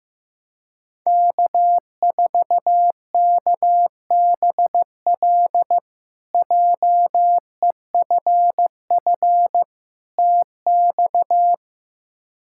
Morse code, heard as a keyed tone, about 15 wpm, 700 Hz. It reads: K4KBL JEFF TX
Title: K1USN SST QSO Practice - 15wpm